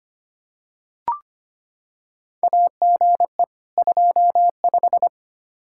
Morse code, heard as a keyed tone, about 25 wpm, 700 Hz. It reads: AGE 25
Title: Practice QSO - Exchange 3 - More Info - 25wpm